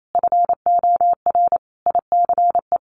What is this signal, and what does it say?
Morse code, heard as a keyed tone, about 28 wpm, 700 Hz. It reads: FOR ICE
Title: Sentences from Top 1500 Words 28wpm